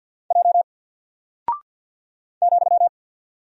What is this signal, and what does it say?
Morse code, heard as a keyed tone, about 48 wpm, 700 Hz. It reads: J ,
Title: Single Character (Letter:Number:Punctuation:Prosign) 50wpm